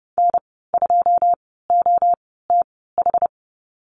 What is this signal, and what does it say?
Morse code, heard as a keyed tone, about 30 wpm, spaced as for 17 wpm, 700 Hz. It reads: N2OTH
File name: Lesson 14g - Number 2 - Call Signs 30wpm